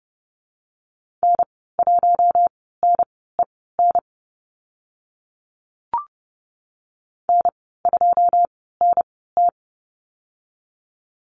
Morse code, heard as a keyed tone, about 30 wpm, spaced as for 17 wpm, 700 Hz. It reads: N1NEN N2NT
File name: Lesson 19g - Letter w - Call Signs 30wpm